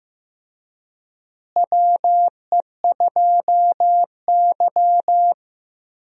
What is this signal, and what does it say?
Morse code, heard as a keyed tone, about 15 wpm, 700 Hz. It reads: WE2Y
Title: USA Call Signs - 4 Characters - Repeated 2x - 15wpm